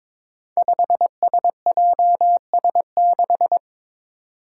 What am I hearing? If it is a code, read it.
Morse code, heard as a keyed tone, about 22 wpm, 700 Hz. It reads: 5SJS6
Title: Five Challenging Letter-Number Combinations 22wpm